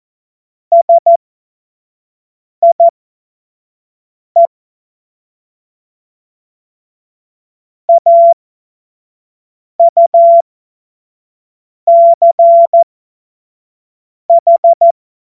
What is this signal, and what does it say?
Morse code, heard as a keyed tone, about 14 wpm, spaced as for 5 wpm, 680 Hz. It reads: SIE AUCH
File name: Morsekurs / Witze / Witz 25. Oktober 2022 25 F70